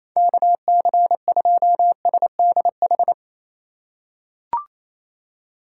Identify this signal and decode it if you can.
Morse code, heard as a keyed tone, about 28 wpm, 690 Hz. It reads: KC2SDH
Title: USA Call Signs - 6 Characters - 4x Character Spacing - 28wpm